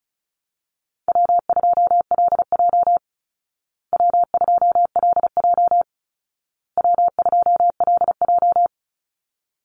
Morse code, heard as a keyed tone, about 34 wpm, 700 Hz. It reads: W2LJ W2LJ W2LJ
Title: Easy Call Signs Repeated 3x - 35wpm